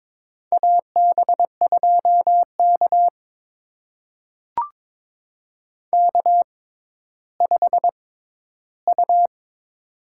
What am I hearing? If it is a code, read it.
Morse code, heard as a keyed tone, about 23 wpm, spaced as for 13 wpm, 700 Hz. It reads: AB2K K5U
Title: USA Call Signs - 4 Characters - 6x Character Spacing - 22wpm